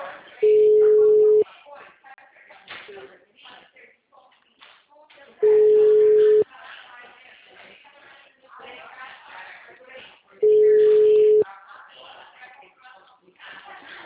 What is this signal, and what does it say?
Ringback tone (Europe, 425 Hz), with speech.
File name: out- -2003- - - .18455